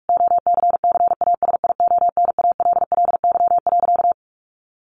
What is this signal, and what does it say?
Morse code, heard as a keyed tone, about 44 wpm, 700 Hz. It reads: OCCASIONALLY.
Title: Sentences from Top 1300 Words 45wpm